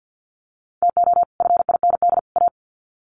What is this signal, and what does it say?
Morse code, heard as a keyed tone, about 50 wpm, 700 Hz. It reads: TO FIND A
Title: Sets of 3 Words 50wpm